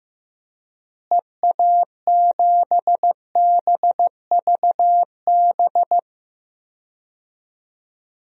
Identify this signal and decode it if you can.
Morse code, heard as a keyed tone, about 15 wpm, 700 Hz. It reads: EA7BVB
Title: DXCC-100 Call Signs - 6 Characters - Repeated 2x - 15wpm